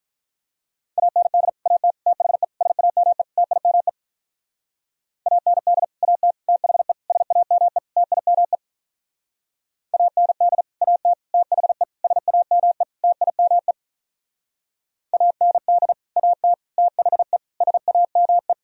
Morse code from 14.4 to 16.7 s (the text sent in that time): AND AT